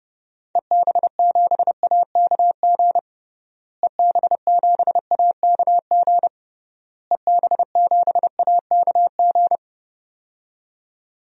Morse code, heard as a keyed tone, about 32 wpm, 700 Hz. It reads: EB7AKG EB7AKG EB7AKG
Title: DXCC-100 Call Signs - 6 Characters - Repeated 3x - 30wpm